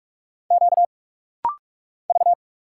Morse code, heard as a keyed tone, about 46 wpm, 700 Hz. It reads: Q V
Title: Single Letter-Number Alt - Rapid-Fire 45wpm